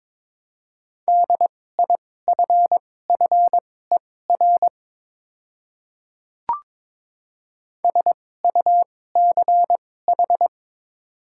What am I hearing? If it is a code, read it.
Morse code, heard as a keyed tone, about 22 wpm, spaced as for 16 wpm, 700 Hz. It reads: DIFFER SUCH
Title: 2x Character Spacing - Top 200 Words - 22wpm